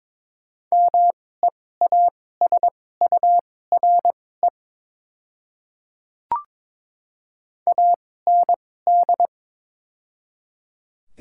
Morse code, heard as a keyed tone, about 22 wpm, spaced as for 16 wpm, 690 Hz. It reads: MEASURE AND
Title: 2x Character Spacing - Top 500 Words - 22wpm